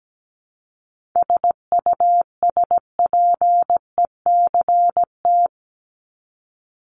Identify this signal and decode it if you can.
Morse code, heard as a keyed tone, about 17 wpm, 700 Hz. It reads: SUSPECT
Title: Top 1751-2000 Words 17wpm